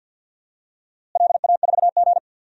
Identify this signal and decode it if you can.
Morse code, heard as a keyed tone, about 52 wpm, 700 Hz. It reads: LA4G
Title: DXCC-100 Call Signs - 4 Characters - Repeated 2x - 50wpm